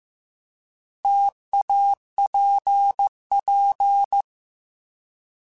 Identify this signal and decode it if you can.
Morse code, heard as a keyed tone, about 15 wpm, 800 Hz. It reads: TAPP
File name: T Prefixes / T mp3s